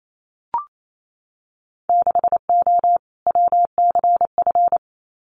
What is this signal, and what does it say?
Morse code, heard as a keyed tone, about 28 wpm, 700 Hz. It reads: 6O WCF
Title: Winter Field Day Exchanges Encore 28wpm